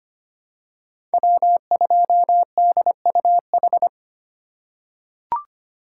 Morse code, heard as a keyed tone, about 25 wpm, 700 Hz. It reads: W2DUH
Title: USA Call Signs - 5 Characters - 3x Character Spacing - 25wpm